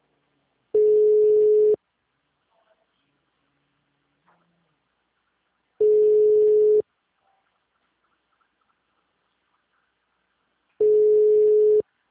Ringback tone (Europe, 425 Hz).
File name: out- -2025- - - .85846